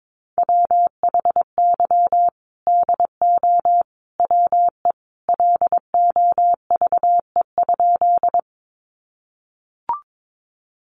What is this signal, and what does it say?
Morse code, heard as a keyed tone, about 22 wpm, 700 Hz. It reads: WHY DO WE LOVE?
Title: Sentences from Top 400 Words 22wpm